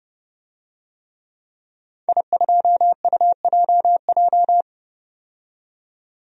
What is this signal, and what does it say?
Morse code, heard as a keyed tone, about 30 wpm, 700 Hz. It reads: I2UJJ
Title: DXCC-100 Call Signs - 5 Characters - 30wpm